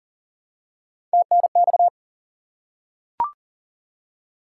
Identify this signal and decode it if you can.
Morse code, heard as a keyed tone, about 40 wpm, 700 Hz. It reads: TNX
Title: Top 90 QSO Elements 40wpm